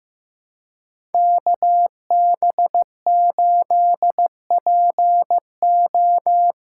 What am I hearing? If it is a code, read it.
Morse code, heard as a keyed tone, about 15 wpm, 700 Hz. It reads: KB8PO
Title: USA Call Signs - 8x Character Spacing - 15wpm